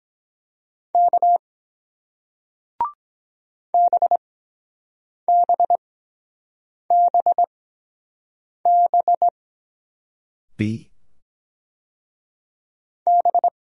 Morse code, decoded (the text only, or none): K BBBB B